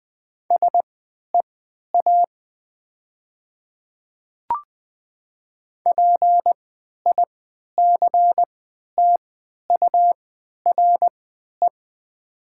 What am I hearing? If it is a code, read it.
Morse code, heard as a keyed tone, about 20 wpm, spaced as for 11 wpm, 690 Hz. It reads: SEA PICTURE
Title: 3x Character Spacing - Top 500 Words - 20wpm